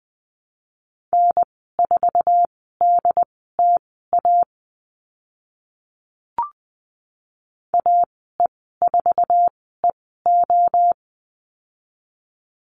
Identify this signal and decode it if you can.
Morse code, heard as a keyed tone, about 20 wpm, spaced as for 14 wpm, 700 Hz. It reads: N4DTA AE4EO
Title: 069 Lesson 12g - Letter d - Call Signs 20wpm